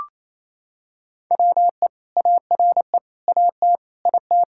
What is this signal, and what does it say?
Morse code, heard as a keyed tone, about 28 wpm, 700 Hz. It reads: WE ARE AT IT
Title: Sentences from Top 1000 Words - Review Part 5 - 28wpm